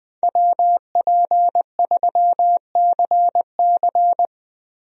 Morse code, heard as a keyed tone, about 20 wpm, 700 Hz. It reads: WP3CC
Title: USA Call Signs - 5 Characters - Repeated 2x - 20wpm